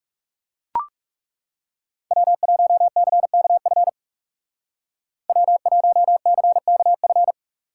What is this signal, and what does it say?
Morse code, heard as a keyed tone, about 37 wpm, 700 Hz. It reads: W1CKF W1CKF
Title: Speed Racing - USA 1x3 Call Signs 30wpm